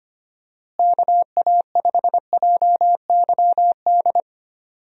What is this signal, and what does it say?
Morse code, heard as a keyed tone, about 25 wpm, 700 Hz. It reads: KA5JYD
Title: USA Call Signs - 6 Characters - Repeated 2x - 25wpm